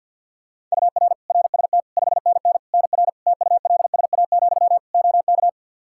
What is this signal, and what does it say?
Morse code, heard as a keyed tone, about 51 wpm, 700 Hz. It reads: UR RST 5NN NR TULSA, OK
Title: Practice QSO - Exchange 1 - Beginning - 50wpm